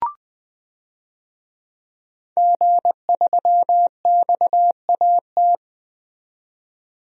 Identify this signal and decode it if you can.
Morse code, heard as a keyed tone, about 20 wpm, 700 Hz. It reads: G3XAT
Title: DXCC-100 Call Signs - 5 Characters - 20wpm